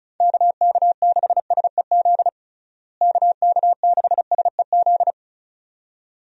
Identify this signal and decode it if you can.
Morse code, heard as a keyed tone, about 36 wpm, 700 Hz. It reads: KK6SEZ KK6SEZ
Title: USA Call Signs - 6 Characters - Repeated 2x - 35wpm